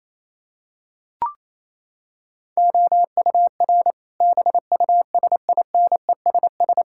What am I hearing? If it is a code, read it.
Morse code, heard as a keyed tone, about 28 wpm, 700 Hz. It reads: OUR BUSINESS
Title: Sentences from Top 1000 Words - Review Part 6 - 28wpm